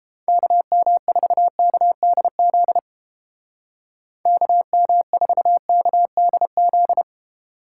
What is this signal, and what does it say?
Morse code, heard as a keyed tone, about 29 wpm, 700 Hz. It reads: KM4KDZ KM4KDZ